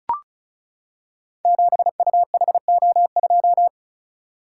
Morse code, heard as a keyed tone, about 36 wpm, 690 Hz. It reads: 7UHO2
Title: Code Groups - 5 Letter:Numbers - 35wpm